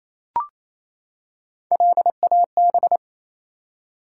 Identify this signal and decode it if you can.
Morse code, heard as a keyed tone, about 29 wpm, 700 Hz. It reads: LAB
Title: Three Letter Words - 28wpm